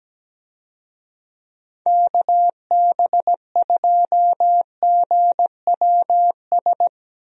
Morse code, heard as a keyed tone, about 17 wpm, 700 Hz. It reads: KB2GWS
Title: USA Call Signs - 6 Characters - Repeated 2x - 17wpm